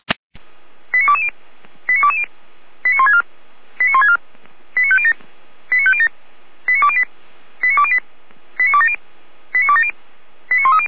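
Five-tone selective call (ZVEI), heard as quick runs of five tones, tones of about 70 ms. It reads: 89220 89220 89164 89164 89507 89507 89208 89208 89270 89270 89130
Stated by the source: Five-tone ZVEI selective call received on an SDR.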